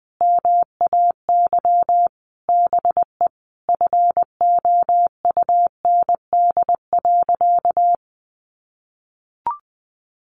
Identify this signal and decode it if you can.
Morse code, heard as a keyed tone, about 20 wpm, 700 Hz. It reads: MAY BE FOUND.
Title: Sentences from Top 1500 Words 20wpm